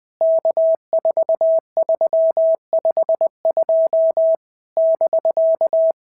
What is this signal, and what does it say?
Morse code, heard as a keyed tone, about 19 wpm, 650 Hz. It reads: K4352 <BK>
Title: POTA QSO Practice - 20wpm